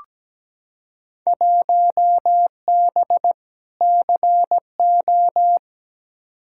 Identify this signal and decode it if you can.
Morse code, heard as a keyed tone, about 17 wpm, 700 Hz. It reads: 1B CO
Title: Field Day Exchanges - 17wpm